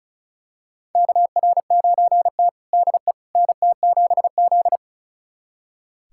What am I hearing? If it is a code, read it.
Morse code, heard as a keyed tone, about 36 wpm, 700 Hz. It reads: KR9T DE NT7Z
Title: USA Callsign DE Callsign - 4 Characters - 35wpm